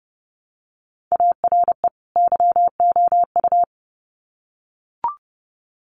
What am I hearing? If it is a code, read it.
Morse code, heard as a keyed tone, about 30 wpm, 700 Hz. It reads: ARE YOU
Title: Sets of 2 Words 30wpm